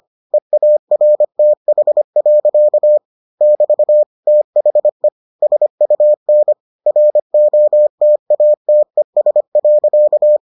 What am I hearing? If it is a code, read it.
Morse code, heard as a keyed tone, about 25 wpm, 600 Hz. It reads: EARTH. <BT> THE SUN ROTATES.